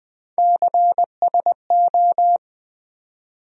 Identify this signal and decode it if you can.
Morse code, heard as a keyed tone, about 20 wpm, 700 Hz. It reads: CSO
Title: USA Call Sign Postfixes - 3 Letters 20wpm